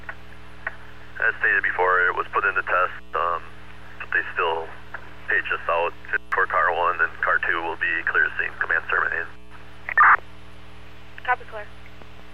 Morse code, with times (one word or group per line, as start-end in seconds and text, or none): none